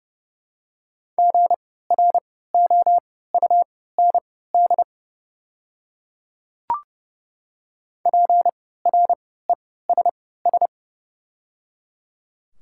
Morse code, heard as a keyed tone, about 30 wpm, spaced as for 17 wpm, 700 Hz. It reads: GROUND PRESS
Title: 3x Character Spacing - Top 500 Words - 30wpm